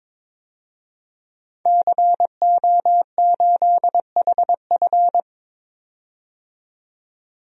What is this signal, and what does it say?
Morse code, heard as a keyed tone, about 22 wpm, 700 Hz. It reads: CO8HF
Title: DXCC-100 Call Signs - 5 Characters - 22wpm